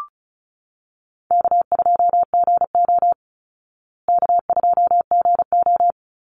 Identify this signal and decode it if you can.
Morse code, heard as a keyed tone, about 34 wpm, 700 Hz. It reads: K2GO K2GO
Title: Call Signs Repeated 2x - 35wpm